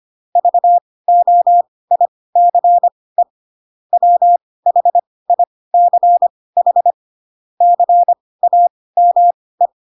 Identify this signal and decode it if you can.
Morse code, heard as a keyed tone, about 25 wpm, spaced as for 18 wpm, 700 Hz. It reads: VOICE WHICH CAME
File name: Pinocchio / Pinocchio ch2 12 25 ik1hge